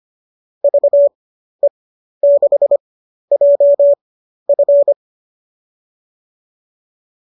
Morse code, heard as a keyed tone, about 25 wpm, spaced as for 12 wpm, 590 Hz. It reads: VE6JF
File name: qso002 12